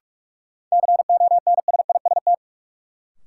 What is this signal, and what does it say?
Morse code, heard as a keyed tone, about 46 wpm, 700 Hz. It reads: CONSIST
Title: Sentences from Top 1400 Words - New Word Drill - 45wpm